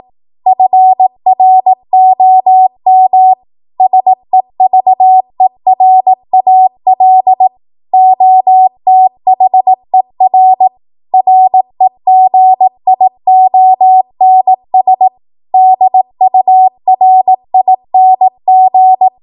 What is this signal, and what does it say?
Morse code, heard as a keyed tone, about 18 wpm, 750 Hz. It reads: FROM SEVERAL OTHER REGIONS DURING
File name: Morse / 18WPM